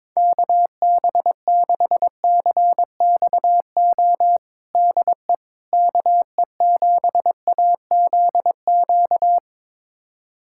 Morse code, heard as a keyed tone, about 22 wpm, 700 Hz. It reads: KB6CXO DE KE7AZQ
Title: USA Callsign DE Callsign - 6 Characters - 22wpm